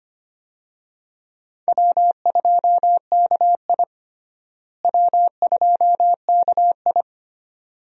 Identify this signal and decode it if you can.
Morse code, heard as a keyed tone, about 25 wpm, 700 Hz. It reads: W2KI W2KI
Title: USA Call Signs - 4 Characters - Repeated 2x - 25wpm